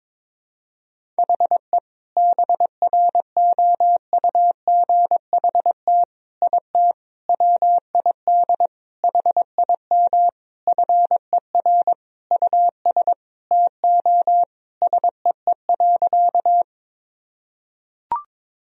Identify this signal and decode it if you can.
Morse code, heard as a keyed tone, about 22 wpm, 700 Hz. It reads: HE BROUGHT IT WID HIM FER US TO SEE.
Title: Sentences from Top 500 Words and Abbreviations 22wpm